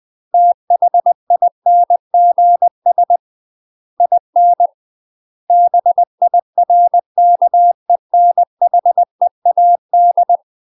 Morse code, heard as a keyed tone, about 20 wpm, 700 Hz. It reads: THINGS IN BIRKENHEAD
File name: 20WPM